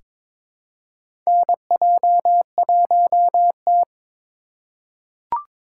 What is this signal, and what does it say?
Morse code, heard as a keyed tone, about 22 wpm, 700 Hz. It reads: NJ1T
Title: K1USN SST QSO Practice - 22wpm